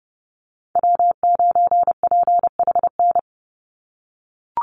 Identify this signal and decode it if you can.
Morse code, heard as a keyed tone, about 30 wpm, 700 Hz. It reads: W9PHN
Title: USA Call Signs 30wpm